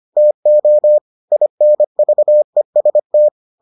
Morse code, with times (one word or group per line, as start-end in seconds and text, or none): 0.2-1.0 TO
1.3-3.3 INVEST